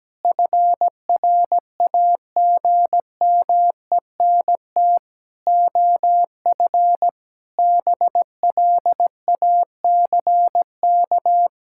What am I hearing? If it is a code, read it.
Morse code, heard as a keyed tone, about 17 wpm, 700 Hz. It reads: FRAGMENT OF BLACK
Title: Pangram Challenge v2 - 17wpm